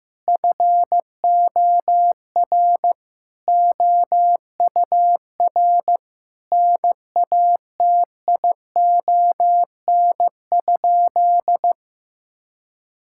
Morse code, heard as a keyed tone, about 15 wpm, 700 Hz. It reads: FOR OUR NATION?
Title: Sentences from Top 1000 Words - Review Part 6 - 15wpm